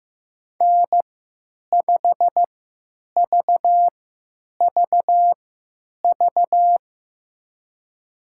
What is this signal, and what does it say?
Morse code, heard as a keyed tone, about 15 wpm, spaced as for 9 wpm, 700 Hz. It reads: N5VVV
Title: USA Call Signs - 5 Characters - 3x Character Spacing - 15wpm